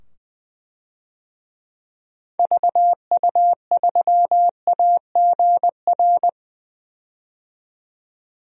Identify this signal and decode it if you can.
Morse code, heard as a keyed tone, about 20 wpm, 700 Hz. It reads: VU3AGR
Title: DXCC-100 Call Signs - 6 Characters - 20wpm